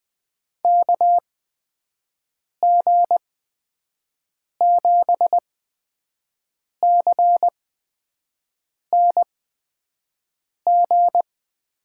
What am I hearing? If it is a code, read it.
Morse code, heard as a keyed tone, about 20 wpm, spaced as for 5 wpm, 700 Hz. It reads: KG7CNG
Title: USA Call Signs - 6 Characters - 8x Character Spacing - 20wpm